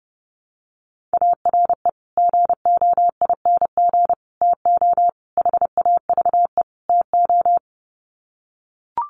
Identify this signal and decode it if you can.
Morse code, heard as a keyed tone, about 28 wpm, 700 Hz. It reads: ARE GOING TO HAVE TO
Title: Sets of 5 Words 30wpm